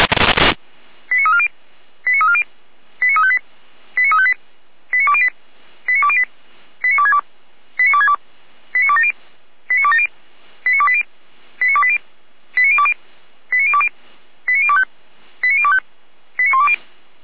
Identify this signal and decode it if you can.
Five-tone selective call (ZVEI), heard as quick runs of five tones, tones of about 70 ms. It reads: 89340 89340 89358 89358 89208 89208 89262 89262 89270 89270 89280 89280 89020 89020 89025 89025 89120